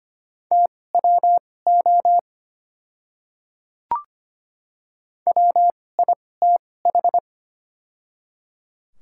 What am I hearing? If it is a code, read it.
Morse code, heard as a keyed tone, about 25 wpm, spaced as for 18 wpm, 700 Hz. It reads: TWO WITH
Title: 2x Character Spacing - Top 100 Words - 25wpm